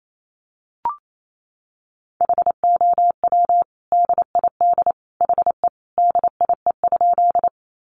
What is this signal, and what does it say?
Morse code, heard as a keyed tone, about 28 wpm, 700 Hz. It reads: HOW DID HE DIE?
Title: Sentences from Top 1300 Words 28wpm